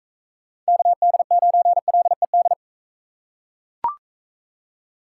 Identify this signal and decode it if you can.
Morse code, heard as a keyed tone, about 41 wpm, 700 Hz. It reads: KD9LED